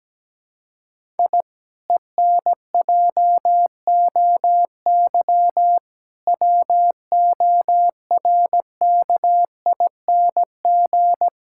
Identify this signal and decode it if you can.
Morse code, heard as a keyed tone, about 17 wpm, 700 Hz. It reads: I ENJOY WORKING
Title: Sentences from Top 1000 Words - Review Part 5 - 17wpm